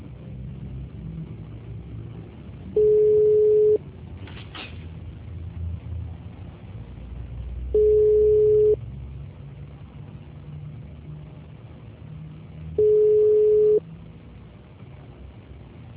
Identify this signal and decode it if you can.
Ringback tone (Europe, 425 Hz).
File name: out- -2018- - - .29773